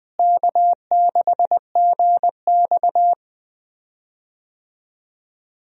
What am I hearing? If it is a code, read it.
Morse code, heard as a keyed tone, about 21 wpm, 700 Hz. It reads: K6GX
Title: USA Call Signs - 4 Characters - Repeated 2x - 20wpm